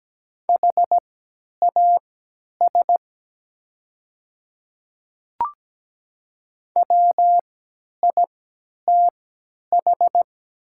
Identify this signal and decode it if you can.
Morse code, heard as a keyed tone, about 17 wpm, spaced as for 10 wpm, 700 Hz. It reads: HAS WITH
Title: 3x Character Spacing - Top 100 Words - 17wpm